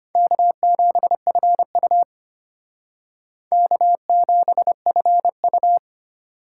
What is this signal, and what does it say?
Morse code, heard as a keyed tone, about 32 wpm, 700 Hz. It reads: K7FU K7FU